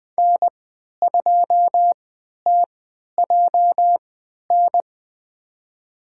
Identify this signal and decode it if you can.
Morse code, heard as a keyed tone, about 20 wpm, spaced as for 11 wpm, 700 Hz. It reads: N2TJN
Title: USA Call Signs - 5 Characters - 3x Character Spacing - 20wpm